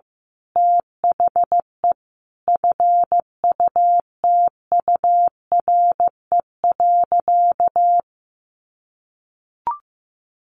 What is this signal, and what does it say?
Morse code, heard as a keyed tone, about 15 wpm, 700 Hz. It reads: THE FUTURE.
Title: A Sound Of Thunder 15wpm